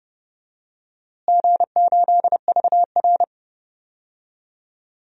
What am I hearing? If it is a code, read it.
Morse code, heard as a keyed tone, about 30 wpm, 700 Hz. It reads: G8VR
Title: DXCC-100 Call Signs - 4 Characters - Repeated 3x - 30wpm